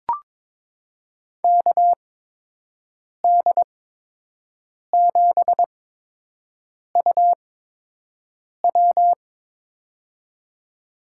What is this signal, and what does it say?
Morse code, heard as a keyed tone, about 23 wpm, spaced as for 6 wpm, 700 Hz. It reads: KD7UW
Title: USA Call Signs - 8x Character Spacing - 22wpm